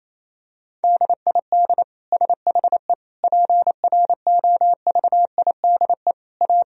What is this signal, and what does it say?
Morse code, heard as a keyed tone, about 28 wpm, 700 Hz. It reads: DID SHE PROVIDE A
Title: Sentences from Top 1000 Words - Review Part 4 - 28wpm